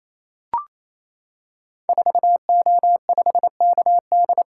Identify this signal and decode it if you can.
Morse code, heard as a keyed tone, about 29 wpm, 700 Hz. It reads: 4O5KD
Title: Code Groups - 5 Letter:Numbers - 28wpm